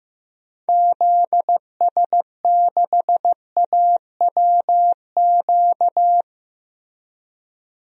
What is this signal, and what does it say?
Morse code, heard as a keyed tone, about 15 wpm, 700 Hz. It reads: ZS6AWQ
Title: DXCC-100 Call Signs - 6 Characters - Repeated 3x - 15wpm